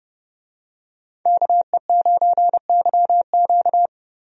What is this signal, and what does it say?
Morse code, heard as a keyed tone, about 31 wpm, 700 Hz. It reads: KE9YQ
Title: USA Call Signs - 5 Characters - Repeated 2x - 30wpm